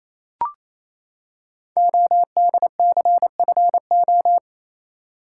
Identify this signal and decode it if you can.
Morse code, heard as a keyed tone, about 28 wpm, 700 Hz. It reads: ODCFO
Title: Code Groups - 5 Letters - 28wpm